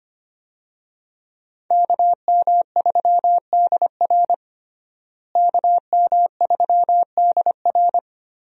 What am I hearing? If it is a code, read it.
Morse code, heard as a keyed tone, about 25 wpm, 700 Hz. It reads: KM3DR KM3DR
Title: USA Call Signs - 5 Characters - Repeated 3x - 25wpm